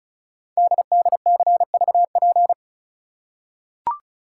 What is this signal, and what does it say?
Morse code, heard as a keyed tone, about 36 wpm, 690 Hz. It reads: DDCVP
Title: Code Groups - 5 Letters - 35wpm